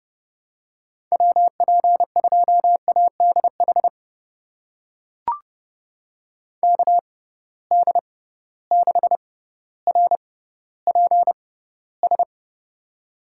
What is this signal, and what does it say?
Morse code, heard as a keyed tone, about 30 wpm, spaced as for 15 wpm, 700 Hz. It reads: WP2ADH KD6RPS
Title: USA Call Signs - 6 Characters - 6x Character Spacing - 30wpm